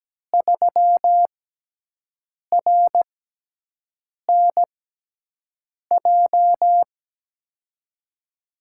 Morse code, heard as a keyed tone, about 17 wpm, spaced as for 6 wpm, 700 Hz. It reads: 3RNJ
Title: USA Call Signs - 5 Characters - 6x Character Spacing - 17wpm